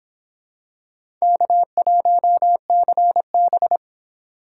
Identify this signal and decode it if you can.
Morse code, heard as a keyed tone, about 26 wpm, 700 Hz. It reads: K1CB